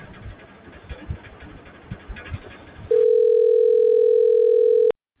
Ringback tone (North America), 440 + 480 Hz.